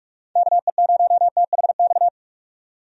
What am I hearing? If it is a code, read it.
Morse code, heard as a keyed tone, about 46 wpm, 700 Hz. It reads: KE0THX
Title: USA Call Signs - 6 Characters - 45wpm